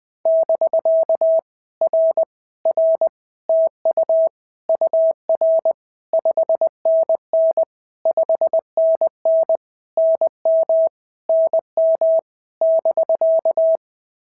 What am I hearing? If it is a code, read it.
Morse code, heard as a keyed tone, about 20 wpm, 650 Hz. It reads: <BK> R R TU UR 5NN 5NN NM NM <BK>
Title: POTA QSO Practice - 20wpm